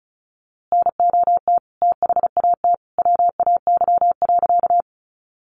Morse code, heard as a keyed tone, about 34 wpm, 700 Hz. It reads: NOT THAT WAY.
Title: A Sound Of Thunder 35wpm